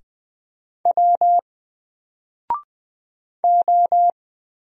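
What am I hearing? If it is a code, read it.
Morse code, heard as a keyed tone, about 20 wpm, 700 Hz. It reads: W O
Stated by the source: Single letter and number Morse drill at 20 wpm, from N4LCD.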